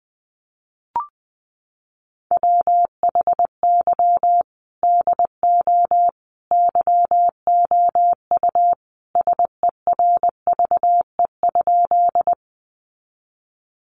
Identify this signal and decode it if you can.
Morse code, heard as a keyed tone, about 20 wpm, 700 Hz. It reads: WHY DO YOU SERVE?
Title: Sentences from Top 400 Words 20wpm